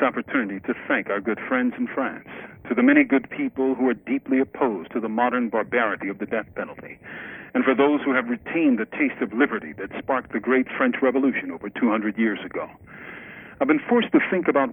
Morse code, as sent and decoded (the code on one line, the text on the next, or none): none